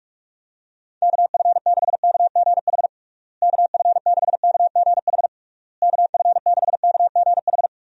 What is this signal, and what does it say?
Morse code, heard as a keyed tone, about 46 wpm, 700 Hz. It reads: KF6KGH KF6KGH KF6KGH
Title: USA Call Signs - 6 Characters - Repeated 3x - 45wpm